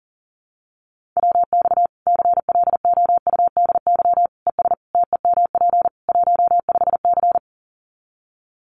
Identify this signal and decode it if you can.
Morse code, heard as a keyed tone, about 40 wpm, 700 Hz. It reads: WX CLOUDY ES TEMP 15C
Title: Common QSO Phrases 40wpm